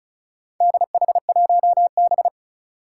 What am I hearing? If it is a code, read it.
Morse code, heard as a keyed tone, about 36 wpm, 700 Hz. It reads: DH1B
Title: Four Challenging Letter-Number Combinations 35wpm